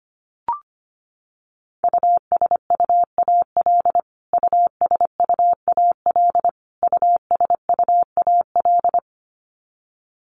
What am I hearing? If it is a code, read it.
Morse code, heard as a keyed tone, about 25 wpm, 700 Hz. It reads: USUAL USUAL USUAL
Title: Top 500 Words Repeated 25wpm